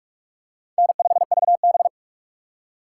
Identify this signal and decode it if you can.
Morse code, heard as a keyed tone, about 46 wpm, 700 Hz. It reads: N5VB
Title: USA Call Signs - 4 Characters - 3x Character Spacing - 45wpm